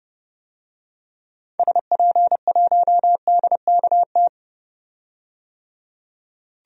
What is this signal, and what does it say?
Morse code, heard as a keyed tone, about 30 wpm, 700 Hz. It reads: SP1DKT
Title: DXCC-100 Call Signs - 6 Characters - 30wpm